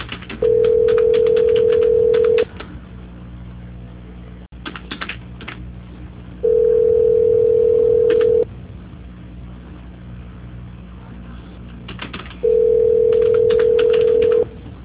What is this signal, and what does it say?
Ringback tone (North America), 440 + 480 Hz.
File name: out- -2045- - - .22416